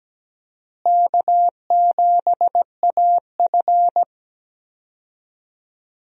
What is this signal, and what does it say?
Morse code, heard as a keyed tone, about 17 wpm, 700 Hz. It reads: K7AF
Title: USA Call Signs - 4 Characters - Repeated 3x - 17wpm